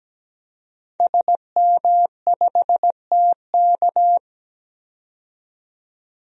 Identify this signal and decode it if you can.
Morse code, heard as a keyed tone, about 17 wpm, 700 Hz. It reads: SM5TK
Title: DXCC-100 Call Signs - 5 Characters - 17wpm